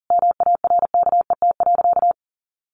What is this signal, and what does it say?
Morse code, heard as a keyed tone, about 40 wpm, 700 Hz. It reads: MARKET.
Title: Sentences from Top 1400 Words 40wpm